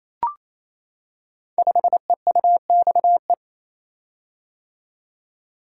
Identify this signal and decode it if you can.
Morse code, heard as a keyed tone, about 28 wpm, 690 Hz. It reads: 5EUXE
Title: Code Groups - 5 Letter:Numbers - 28wpm